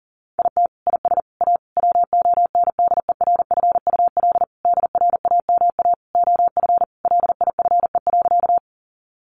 Morse code, heard as a keyed tone, about 40 wpm, 700 Hz. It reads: IT IS A WONDERFUL DRAMA OF LIFE.
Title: Sentences from Top 1500 Words 40wpm